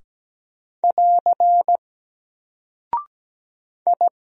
Morse code, heard as a keyed tone, about 17 wpm, 700 Hz. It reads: <AR> I
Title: Single Character (Letter:Number:Punctuation:Prosign) 17wpm